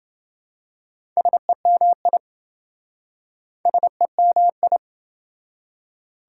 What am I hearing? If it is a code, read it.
Morse code, heard as a keyed tone, about 26 wpm, 700 Hz. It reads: SEMI SEMI